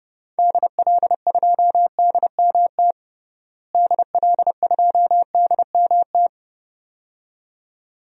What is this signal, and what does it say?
Morse code, heard as a keyed tone, about 30 wpm, 700 Hz. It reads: DL2DMT DL2DMT
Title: DXCC-100 Call Signs - 6 Characters - Repeated 2x - 30wpm